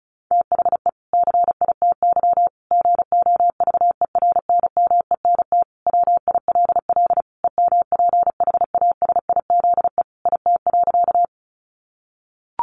Morse code, heard as a keyed tone, about 34 wpm, 700 Hz. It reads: THE CITY GOVERNMENT WILL EMPHASIZE IT.
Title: Sentences from Top 1500 Words 35wpm